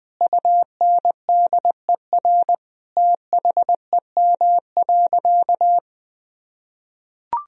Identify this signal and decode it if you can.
Morse code, heard as a keyed tone, about 21 wpm, 700 Hz. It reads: UNDER THEM.
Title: Sentences from Top 1000 Words - Review Part 1 - 20wpm